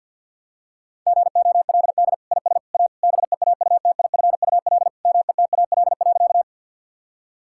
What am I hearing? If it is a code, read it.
Morse code, heard as a keyed tone, about 53 wpm, 700 Hz. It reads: GOLD IS A BEAUTIFUL METAL.
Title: Sentences from Top 1000 Words - Review Part 6 - 50wpm